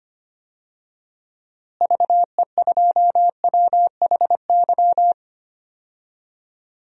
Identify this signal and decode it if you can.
Morse code, heard as a keyed tone, about 25 wpm, 700 Hz. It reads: VE2WHY
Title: DXCC-100 Call Signs - 6 Characters - Repeated 3x - 25wpm